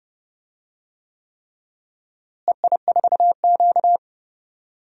Morse code, heard as a keyed tone, about 30 wpm, 700 Hz. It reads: EI4Q
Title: DXCC-100 Call Signs - 4 Characters - Repeated 3x - 30wpm